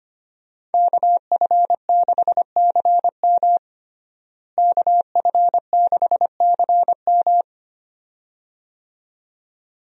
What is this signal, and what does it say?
Morse code, heard as a keyed tone, about 25 wpm, 700 Hz. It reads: KF6CM KF6CM
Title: USA Call Signs - 5 Characters - Repeated 2x - 25wpm